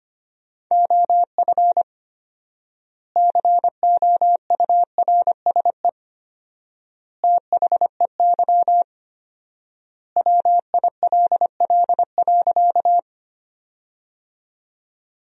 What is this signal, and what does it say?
Morse code, heard as a keyed tone, about 25 wpm, 700 Hz. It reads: OF COURSE THEY WILL.
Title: 4x Word-Spacing - Sentences from Top 500 Words - 25wpm